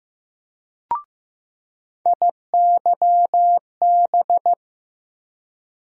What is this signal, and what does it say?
Morse code, heard as a keyed tone, about 15 wpm, 700 Hz. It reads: IYB
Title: USA Call Sign Postfixes - 3 Letters 15wpm